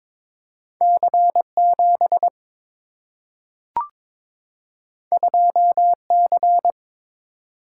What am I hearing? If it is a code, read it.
Morse code, heard as a keyed tone, about 22 wpm, 700 Hz. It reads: C 7 2 C
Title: Single-Letter Single-Number 22wpm